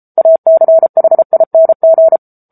Morse code, heard as a keyed tone, about 34 wpm, 650 Hz. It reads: ACHING
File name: Book Two / 07-Chapter 7 - 33 WPM